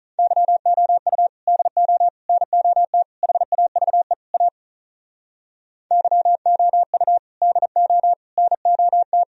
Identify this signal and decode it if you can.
Morse code, heard as a keyed tone, about 46 wpm, 700 Hz. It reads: YOU DO NOT HAVE A YOU DO NOT